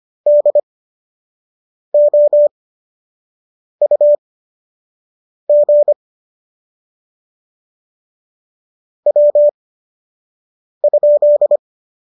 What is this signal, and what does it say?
Morse code, heard as a keyed tone, about 25 wpm, spaced as for 6 wpm, 590 Hz. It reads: DOUG W?